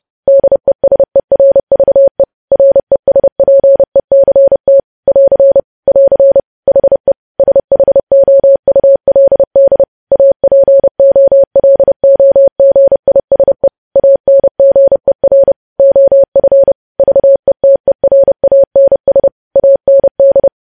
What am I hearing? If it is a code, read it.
Morse code, heard as a keyed tone, about 30 wpm, 550 Hz. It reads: DESERVE RESPECT <AR> <AR> HE SHOULD APOLOGISE ANGER OF VETERANS AND